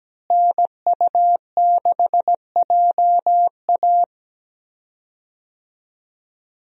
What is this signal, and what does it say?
Morse code, heard as a keyed tone, about 17 wpm, 700 Hz. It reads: NU6JA
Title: USA Call Signs - 5 Characters - Repeated 2x - 17wpm